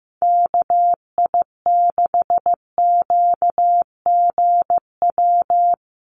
Morse code, heard as a keyed tone, about 15 wpm, 700 Hz. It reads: KI6QGW
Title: USA Call Signs 15wpm